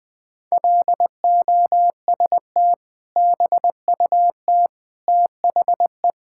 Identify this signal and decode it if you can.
Morse code, heard as a keyed tone, about 20 wpm, 700 Hz. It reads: LOST BUT THE
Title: Sentences with 2-Word Phrases - 20wpm